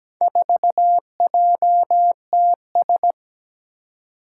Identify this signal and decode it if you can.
Morse code, heard as a keyed tone, about 17 wpm, 700 Hz. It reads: 4JTS
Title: Four Challenging Letter-Number Combinations 17wpm